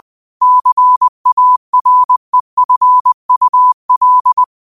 Morse code, heard as a keyed tone, about 20 wpm, 1000 Hz. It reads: CAREFUL
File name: qotdfiles / 20wpm